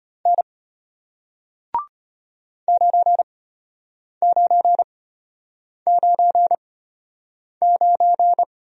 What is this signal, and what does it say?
Morse code, heard as a keyed tone, about 33 wpm, spaced as for 8 wpm, 700 Hz. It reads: N 9999